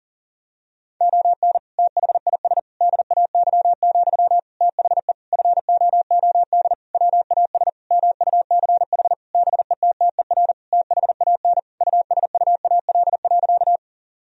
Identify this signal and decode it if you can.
Morse code, heard as a keyed tone, about 40 wpm, 700 Hz. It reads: ON THIS DAY, THE FOOD WAS MUCH BETTER THAN USUAL.
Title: Sentences from Top 1000 Words - Review Part 2 - 40wpm